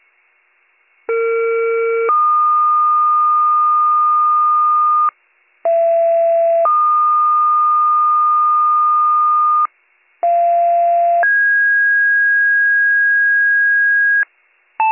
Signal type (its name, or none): two-tone page